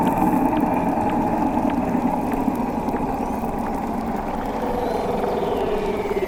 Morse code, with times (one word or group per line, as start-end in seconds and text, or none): none